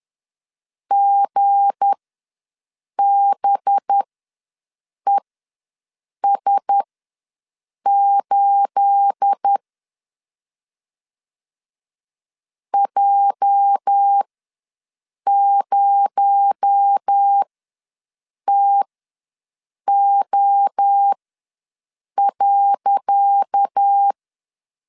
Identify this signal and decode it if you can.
Morse code, heard as a keyed tone, about 10 wpm, spaced as for 6 wpm, 800 Hz. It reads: GBES8 J0TO.
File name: LEZIONI / 10 SIGNS / 10C DOT ws 10wpm 4213